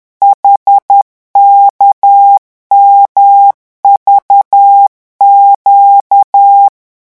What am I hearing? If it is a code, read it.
Morse code, heard as a keyed tone, about 10 wpm, 800 Hz. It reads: HKMVQ